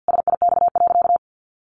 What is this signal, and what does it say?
Morse code, heard as a keyed tone, about 50 wpm, 700 Hz. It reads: SIX.
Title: Sentences from Top 600 Words 50wpm